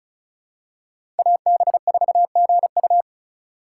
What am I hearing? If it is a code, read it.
Morse code, heard as a keyed tone, about 36 wpm, 700 Hz. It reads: AB4GU
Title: USA Call Signs - 5 Characters - 2x Character Spacing - 35wpm